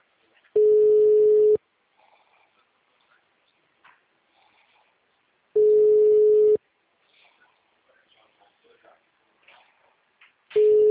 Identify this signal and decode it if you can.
Ringback tone (Europe, 425 Hz), with speech.